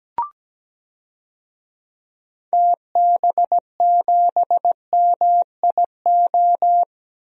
Morse code, heard as a keyed tone, about 17 wpm, 700 Hz. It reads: TB7MIO
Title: DXCC-100 Call Signs - 6 Characters - Repeated 2x - 17wpm